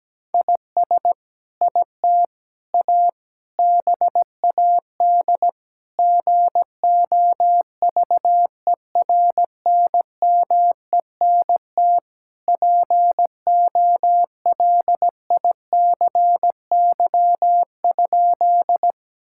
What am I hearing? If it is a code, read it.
Morse code, heard as a keyed tone, about 17 wpm, 700 Hz. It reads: IS IT A BAD GOVERNMENT POLICY?
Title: Sentences from Top 1000 Words - Review Part 5 - 17wpm